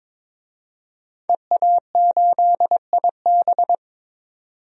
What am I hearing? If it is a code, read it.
Morse code, heard as a keyed tone, about 22 wpm, 700 Hz. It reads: EA8IB
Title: DXCC-100 Call Signs - 5 Characters - 22wpm